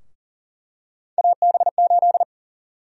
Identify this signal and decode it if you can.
Morse code, heard as a keyed tone, about 38 wpm, 700 Hz. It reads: AB8
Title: USA Call Sign Prefixes - Repeated 2x - 40wpm